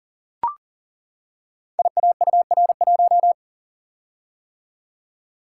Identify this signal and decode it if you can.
Morse code, heard as a keyed tone, about 38 wpm, 690 Hz. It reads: IAUR1
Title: Code Groups - 5 Letter:Numbers - 40wpm